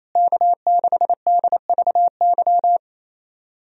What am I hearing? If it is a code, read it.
Morse code, heard as a keyed tone, about 27 wpm, 700 Hz. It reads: K6DVY
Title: USA Call Signs - 5 Characters - Repeated 3x - 28wpm